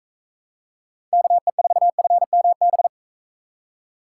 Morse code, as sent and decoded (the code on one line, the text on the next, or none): -.- . ....- ..-. -- -...
KE4FMB